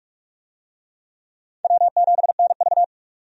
Morse code, heard as a keyed tone, about 46 wpm, 700 Hz. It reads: W7NV
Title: USA Call Signs - 4 Characters - 45wpm